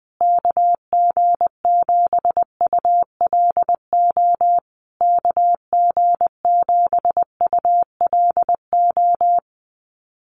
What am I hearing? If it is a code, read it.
Morse code, heard as a keyed tone, about 20 wpm, 700 Hz. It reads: KG7ULO KG7ULO
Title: USA Call Signs Repeated 3x 20wpm